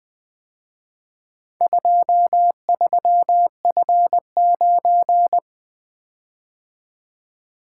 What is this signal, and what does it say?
Morse code, heard as a keyed tone, about 24 wpm, 700 Hz. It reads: 23F9
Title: Speed-Racing - Four Letter-Number Combinations - 20wpm